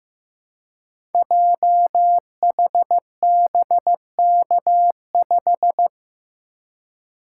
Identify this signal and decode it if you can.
Morse code, heard as a keyed tone, about 15 wpm, 700 Hz. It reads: JHBK5
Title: Code Groups - 5 Letter:Numbers - 15wpm